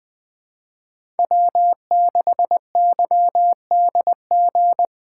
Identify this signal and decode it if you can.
Morse code, heard as a keyed tone, about 21 wpm, 700 Hz. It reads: W6YDG
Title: Speed Racing - USA 1x3 Call Signs 17wpm